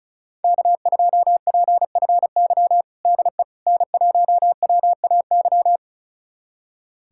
Morse code, heard as a keyed tone, about 34 wpm, 700 Hz. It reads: K2PFY DE N1WAY
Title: USA Callsign DE Callsign 35wpm